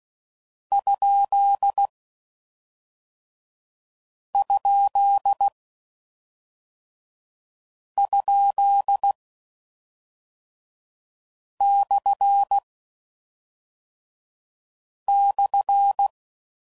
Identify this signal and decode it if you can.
Morse code, heard as a keyed tone, about 16 wpm, 800 Hz. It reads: ? ? ? / /
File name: Mixed / Misc / Punct